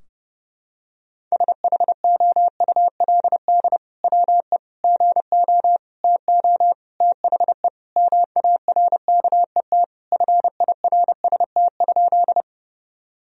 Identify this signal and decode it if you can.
Morse code, heard as a keyed tone, about 32 wpm, 700 Hz. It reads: SHOULD WE GO TO THE MARKET FIRST?
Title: Sentences from Top 1000 Words - Review Part 3 - 30wpm